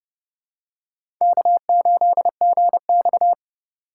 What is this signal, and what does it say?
Morse code, heard as a keyed tone, about 32 wpm, 700 Hz. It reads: K8GX
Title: USA Call Signs - 4 Characters - 3x Character Spacing - 30wpm